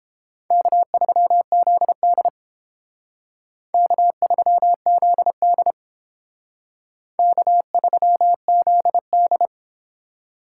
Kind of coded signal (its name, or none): Morse code